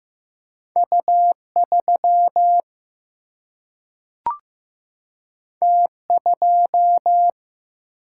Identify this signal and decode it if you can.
Morse code, heard as a keyed tone, about 15 wpm, 700 Hz. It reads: U 3 T 2
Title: Single-Letter Single-Number 15wpm